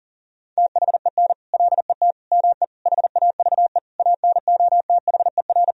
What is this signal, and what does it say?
Morse code, heard as a keyed tone, about 40 wpm, 700 Hz. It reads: THEN LET ME HAVE ANOTHER
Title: Sentences from Top 1000 Words - Review Part 2 - 40wpm